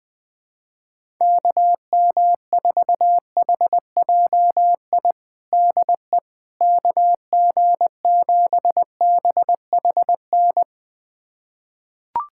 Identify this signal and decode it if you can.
Morse code, heard as a keyed tone, about 21 wpm, 700 Hz. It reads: KM4HJI DE KG7BHN
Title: USA Callsign DE Callsign - 6 Characters - 20wpm